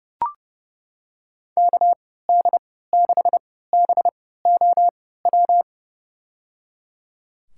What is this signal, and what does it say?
Morse code, heard as a keyed tone, about 30 wpm, spaced as for 17 wpm, 700 Hz. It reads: KD6BOW
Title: USA Call Signs - 6 Characters - 3x Character Spacing - 30wpm